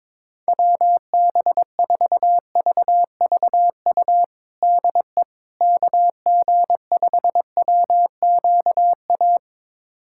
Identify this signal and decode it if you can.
Morse code, heard as a keyed tone, about 22 wpm, 700 Hz. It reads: WB4VVU DE KG5WQA
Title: USA Callsign DE Callsign - 6 Characters - 22wpm